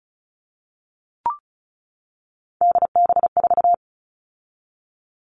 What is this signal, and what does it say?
Morse code, heard as a keyed tone, about 34 wpm, 700 Hz. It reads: DB4
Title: All Cut Numbers - 3-Digit 35wpm